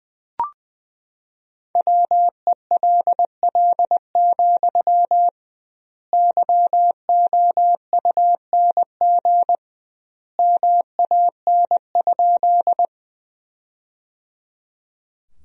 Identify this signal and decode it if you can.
Morse code, heard as a keyed tone, about 20 wpm, 700 Hz. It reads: WELL, YOUNG MAN?
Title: 2x Word-Spacing - Sentences from Top 400 Words - 20wpm